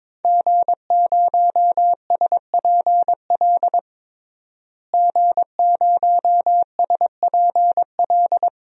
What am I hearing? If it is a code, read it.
Morse code, heard as a keyed tone, about 22 wpm, 700 Hz. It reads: G0SPL G0SPL
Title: DXCC-100 Call Signs - 5 Characters - Repeated 3x - 22wpm